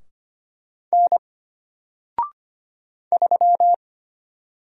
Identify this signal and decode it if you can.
Morse code, heard as a keyed tone, about 25 wpm, 700 Hz. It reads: N 3
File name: Single-Letter-Number-25wpm